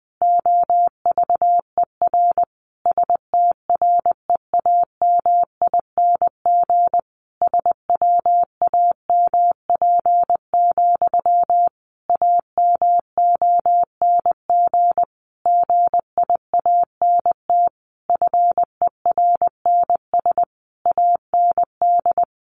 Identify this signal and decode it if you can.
Morse code, heard as a keyed tone, about 20 wpm, 700 Hz. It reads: OVER STREAMING SWAMP, AMONG GIANT FERNS AND
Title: A Sound Of Thunder 20wpm